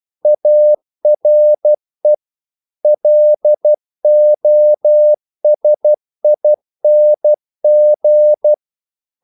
Morse code, heard as a keyed tone, about 12 wpm, 600 Hz. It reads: ARE LOSING
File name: CW / Practice / If12